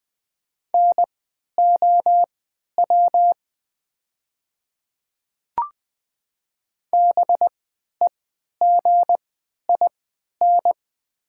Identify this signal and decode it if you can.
Morse code, heard as a keyed tone, about 20 wpm, spaced as for 11 wpm, 700 Hz. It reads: NOW BEGIN
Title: 3x Character Spacing - Top 300 Words - 20wpm